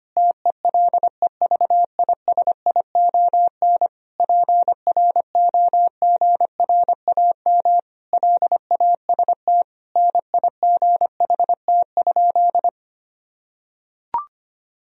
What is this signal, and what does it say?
Morse code, heard as a keyed tone, about 25 wpm, 700 Hz. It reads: TELEVISION PROGRAM LAST NIGHT?
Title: Sentences from Top 1000 Words - Review Part 4 - 25wpm